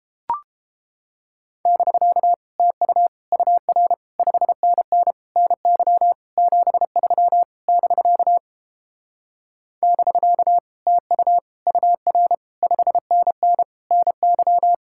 Morse code, heard as a keyed tone, about 31 wpm, 700 Hz. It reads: <BK> TU UR 5NN NY 73 <BK> <BK> TU UR 5NN NY